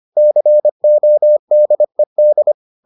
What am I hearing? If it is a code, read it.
Morse code, heard as a keyed tone, about 25 wpm, 600 Hz. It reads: CODED